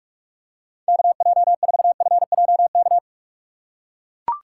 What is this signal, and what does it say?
Morse code, heard as a keyed tone, about 41 wpm, 700 Hz. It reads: KJ4FJK
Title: USA Call Signs - 6 Characters - 6x Character Spacing - 45wpm